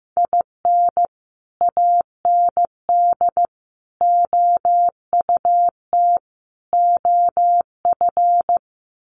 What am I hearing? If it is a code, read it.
Morse code, heard as a keyed tone, about 15 wpm, 700 Hz. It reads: IN AND OUT OF
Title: Sets of 4 Words 15wpm